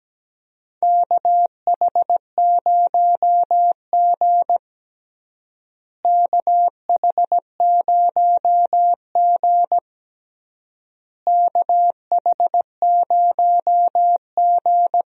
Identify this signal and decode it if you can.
Morse code, heard as a keyed tone, about 17 wpm, 700 Hz. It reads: KH0G KH0G KH0G
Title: USA Call Signs - 4 Characters - Repeated 3x - 17wpm